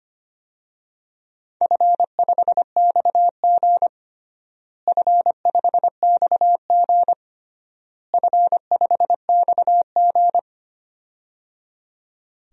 Morse code, heard as a keyed tone, about 25 wpm, 700 Hz. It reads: F5XG F5XG F5XG
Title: DXCC-100 Call Signs - 4 Characters - Repeated 3x - 25wpm